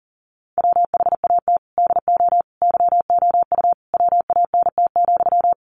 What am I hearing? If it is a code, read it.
Morse code, heard as a keyed tone, about 40 wpm, 700 Hz. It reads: WHAT DO YOU WANT,
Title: Sentences from Top 1000 Words 40wpm